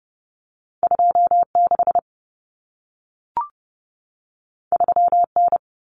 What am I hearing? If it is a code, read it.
Morse code, heard as a keyed tone, about 30 wpm, 700 Hz. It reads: 26 3N
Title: Common Cut Numbers - 2-Digit 30wpm